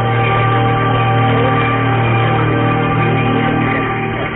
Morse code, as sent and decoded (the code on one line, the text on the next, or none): none